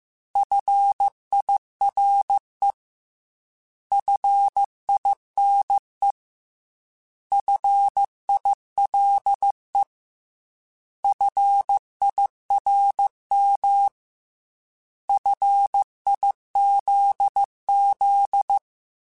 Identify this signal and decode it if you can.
Morse code, heard as a keyed tone, about 15 wpm, 800 Hz. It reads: FIRE FINE FILE FIRM FIZZ
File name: Alphabetical / Alphabetical mp3s / Fs @13